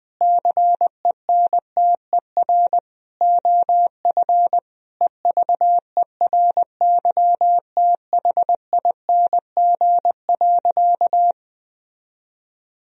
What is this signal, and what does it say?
Morse code, heard as a keyed tone, about 20 wpm, 700 Hz. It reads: CENTER OF EVERYTHING.
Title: Sentences from Top 1000 Words - Review Part 2 - 20wpm